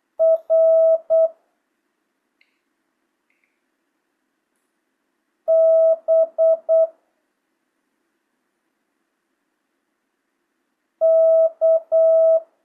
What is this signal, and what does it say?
Morse code, heard as a keyed tone, about 8 wpm, 650 Hz. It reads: R B K
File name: jh8cbh / 2025cw